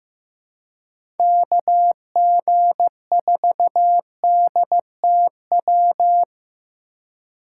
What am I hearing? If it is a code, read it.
Morse code, heard as a keyed tone, about 15 wpm, 700 Hz. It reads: KG4DTW
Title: USA Call Signs - 6 Characters - 3x Character Spacing - 15wpm